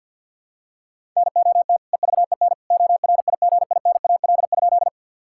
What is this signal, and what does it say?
Morse code, heard as a keyed tone, about 52 wpm, 700 Hz. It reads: NOT EVEN ORIGINAL?
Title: Sentences from Top 1000 Words - Review Part 6 - 50wpm